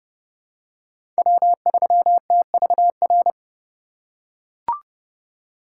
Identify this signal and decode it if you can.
Morse code, heard as a keyed tone, about 31 wpm, 700 Hz. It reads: W3TVR
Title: USA Call Signs - 5 Characters - 2x Character Spacing - 30wpm